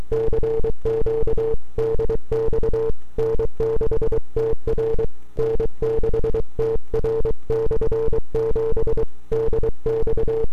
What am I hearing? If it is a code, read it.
Morse code, heard as a keyed tone, about 23 wpm, 450 Hz. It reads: CQ DX N6TR N6TR/7 DX
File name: n6tr / 12Oct2009-160